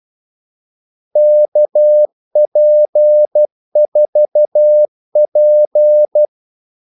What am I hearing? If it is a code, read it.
Morse code, heard as a keyed tone, about 12 wpm, 600 Hz. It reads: KP4P
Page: 2 ekstra ordmellemrum mellem kaldesignalerne